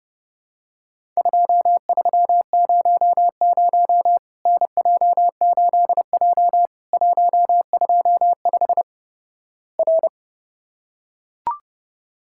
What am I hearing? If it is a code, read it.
Morse code, heard as a keyed tone, about 32 wpm, 700 Hz. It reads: 2300 NJ8J 125 R
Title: Worked All Europe DX Contest - QTC Exchange - 30wpm